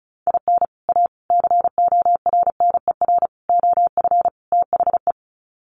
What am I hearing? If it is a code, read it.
Morse code, heard as a keyed tone, about 34 wpm, 700 Hz. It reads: IN A CORNER OF THE
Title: Sets of 5 Words Encore2 35wpm